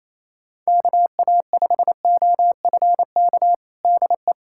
Morse code, heard as a keyed tone, about 28 wpm, 700 Hz. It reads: KA5OFK DE
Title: USA Callsign DE Callsign - 6 Characters - 28wpm